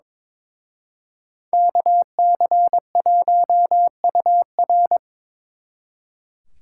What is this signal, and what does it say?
Morse code, heard as a keyed tone, about 26 wpm, 700 Hz. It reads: KC1UR
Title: Speed Racing - USA Call Signs 22wpm